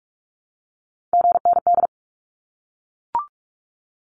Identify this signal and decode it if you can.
Morse code, heard as a keyed tone, about 44 wpm, 700 Hz. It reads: GND
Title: Abbreviations Repeated 3x - 45wpm